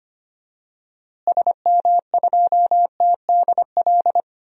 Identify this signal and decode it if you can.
Morse code, heard as a keyed tone, about 25 wpm, 700 Hz. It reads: SM2TDL
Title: DXCC-100 Call Signs - 6 Characters - Repeated 3x - 25wpm